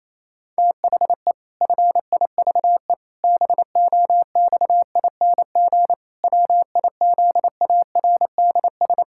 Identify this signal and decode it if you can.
Morse code, heard as a keyed tone, about 28 wpm, 700 Hz. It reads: THE FIVE BOXING WIZARDS
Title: Pangram Challenge v2 - 28wpm